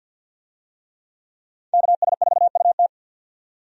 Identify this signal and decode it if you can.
Morse code, heard as a keyed tone, about 50 wpm, 700 Hz. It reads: KS4UT
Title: USA Call Signs - 5 Characters - 50wpm